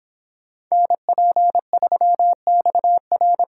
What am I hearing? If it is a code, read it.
Morse code, heard as a keyed tone, about 26 wpm, 700 Hz. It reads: NP3XR